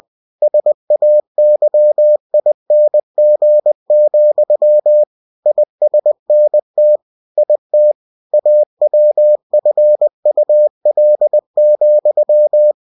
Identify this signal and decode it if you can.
Morse code, heard as a keyed tone, about 20 wpm, 600 Hz. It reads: SAYING, ISNT IT AWFUL,
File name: cw-books / winnie-the-pooh / winnie-20wpm-page-0091